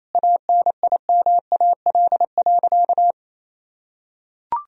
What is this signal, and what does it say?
Morse code, heard as a keyed tone, about 28 wpm, 700 Hz. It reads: ANIMAL.
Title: Sentences from Top 1000 Words - Review Part 1 - 28wpm